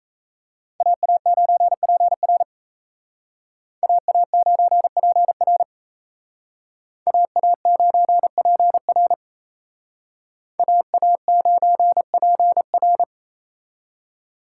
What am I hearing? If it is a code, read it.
Morse code, heard as a keyed tone, about 33 wpm, 700 Hz. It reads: AA9PR AA9PR AA9PR AA9PR
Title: Speed Racing - USA 2x2 Call Signs 28wpm